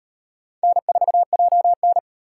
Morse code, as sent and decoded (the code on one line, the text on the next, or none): -. ....- .--- -.
N4JN